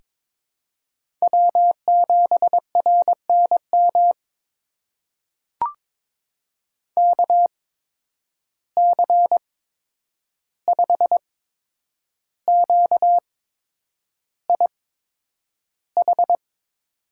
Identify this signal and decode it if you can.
Morse code, heard as a keyed tone, about 22 wpm, spaced as for 9 wpm, 700 Hz. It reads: W7RNM KC5QIH
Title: USA Call Signs - 8x Character Spacing - 22wpm